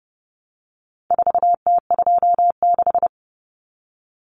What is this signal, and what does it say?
Morse code, heard as a keyed tone, about 30 wpm, 700 Hz. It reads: 4T26
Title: Common Cut Numbers - 4-Digit 30wpm